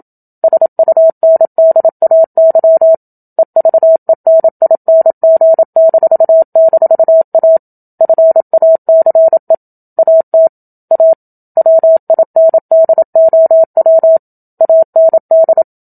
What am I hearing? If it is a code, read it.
Morse code, heard as a keyed tone, about 27 wpm, 650 Hz. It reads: SUNDAY EVENING--A FACE AT A WINDOW AND
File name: Book Two / 01-Chapter 1 - 27 WPM